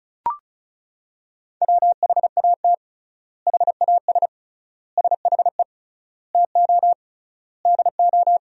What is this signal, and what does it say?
Morse code, heard as a keyed tone, about 34 wpm, 700 Hz. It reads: WHAT HAS SHE TO DO
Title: 3x Word-Spacing - Sentences from Top 100 Words - 35wpm